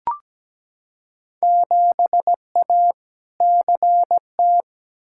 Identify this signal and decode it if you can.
Morse code, heard as a keyed tone, about 17 wpm, 700 Hz. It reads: 7A CT
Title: Field Day Exchanges - 17wpm